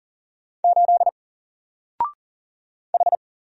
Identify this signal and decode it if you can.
Morse code, heard as a keyed tone, about 41 wpm, 700 Hz. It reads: 8 H
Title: Single Character (Letter:Number:Punctuation:Prosign) 40wpm